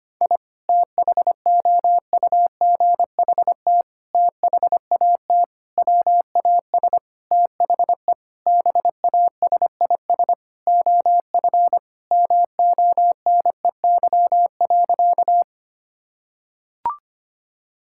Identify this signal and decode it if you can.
Morse code, heard as a keyed tone, about 25 wpm, 700 Hz. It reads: I THOUGHT THAT WAS THE BASIS OF MONEY.
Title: Sentences from Top 1000 Words - Review Part 5 - 25wpm